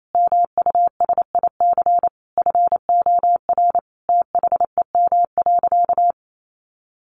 Morse code, heard as a keyed tone, about 28 wpm, 700 Hz. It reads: MUSIC FOR THEM.
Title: Sentences from Top 300 Words 28wpm